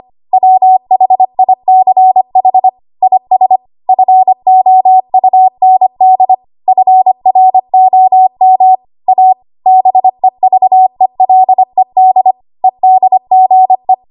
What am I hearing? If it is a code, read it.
Morse code, heard as a keyed tone, about 25 wpm, 750 Hz. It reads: WHICH IS FOUND FROM A BEVELED EDGE
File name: Archive / 25 WPM / 25WPM